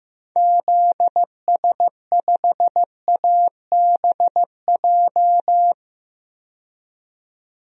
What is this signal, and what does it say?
Morse code, heard as a keyed tone, about 16 wpm, 700 Hz. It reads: ZS5ABJ
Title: DXCC-100 Call Signs - 6 Characters - 15wpm